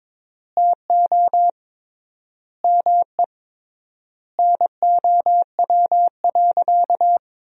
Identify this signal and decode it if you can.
Morse code, heard as a keyed tone, about 22 wpm, 700 Hz. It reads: TO ME NOW.
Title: 3x Word-Spacing - Sentences from Top 500 Words - 22wpm